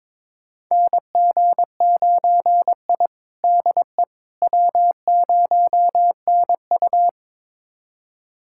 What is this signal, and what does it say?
Morse code, heard as a keyed tone, about 22 wpm, 700 Hz. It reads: NG9I DE W0NU
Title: USA Callsign DE Callsign - 4 Characters - 22wpm